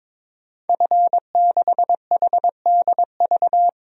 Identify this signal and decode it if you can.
Morse code, heard as a keyed tone, about 22 wpm, 700 Hz. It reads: F6HDV
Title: DXCC-100 Call Signs - 5 Characters - 22wpm